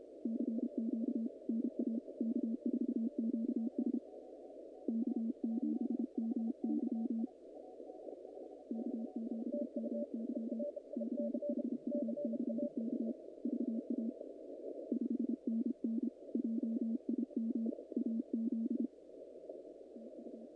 Morse code, heard as a keyed tone, about 30 wpm, 220 Hz. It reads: CQ NA K4QS K7MY K7MY CHUCK VA 5NN JIM AZ TU
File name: contests / naqp-cw-k5tr-op-k5pi- -2100